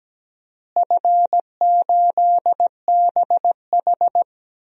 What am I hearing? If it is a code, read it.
Morse code, heard as a keyed tone, about 17 wpm, 700 Hz. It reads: F8BH
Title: DXCC-100 Call Signs - 4 Characters - Repeated 3x - 17wpm